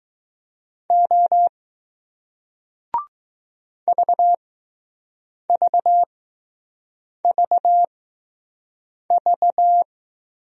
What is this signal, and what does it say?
Morse code, heard as a keyed tone, about 20 wpm, spaced as for 6 wpm, 700 Hz. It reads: O VVVV